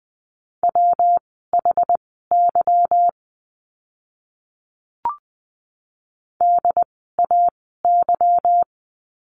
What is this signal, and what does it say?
Morse code, heard as a keyed tone, about 20 wpm, spaced as for 15 wpm, 700 Hz. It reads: WHY DAY
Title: 206 Lesson 33f - Letter j - Words 20wpm